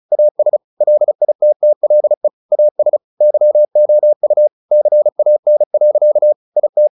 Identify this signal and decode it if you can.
Morse code, heard as a keyed tone, about 34 wpm, 600 Hz. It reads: AS LITTLE AS YOU CAN. IT
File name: the old man and the sea / the old man and the sea 1 35wpm